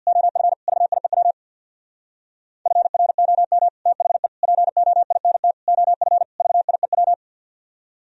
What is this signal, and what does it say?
Morse code, heard as a keyed tone, about 52 wpm, 690 Hz. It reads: OF VIEW FROM THE POINT OF VIEW
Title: Speed-Racing - Sets of 5 Words 50wpm